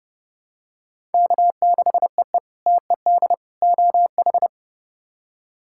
Morse code, heard as a keyed tone, about 29 wpm, 700 Hz. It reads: K6EE TED OH
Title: K1USN SST QSO Practice - 30wpm